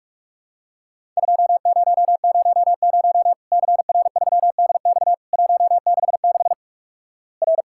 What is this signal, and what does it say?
Morse code, heard as a keyed tone, about 46 wpm, 700 Hz. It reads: 2000 CR3DX 166 R
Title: Worked All Europe DX Contest - QTC Exchange - 45wpm